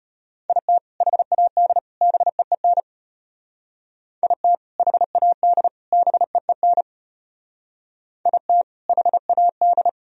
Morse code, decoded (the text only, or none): IT HAD BEEN IT HAD BEEN IT HAD